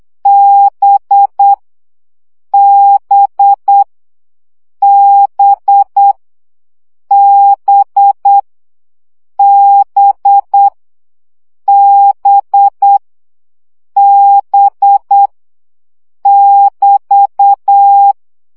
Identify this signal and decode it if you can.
Morse code, heard as a keyed tone, about 8 wpm, 800 Hz. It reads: B B B B B B B <BT>
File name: CW gyakorlas b 7 baud